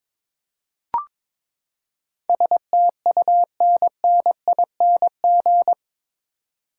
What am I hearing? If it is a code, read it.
Morse code, heard as a keyed tone, about 22 wpm, 700 Hz. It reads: STUNNING
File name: practice / 22wpm / Eight Letter Words - 22wpm